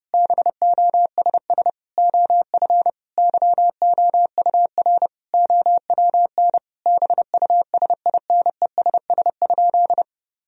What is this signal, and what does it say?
Morse code, heard as a keyed tone, about 32 wpm, 700 Hz. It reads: BOSS OF YOUR OWN BUSINESS?
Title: Sentences from Top 1000 Words - Review Part 4 - 30wpm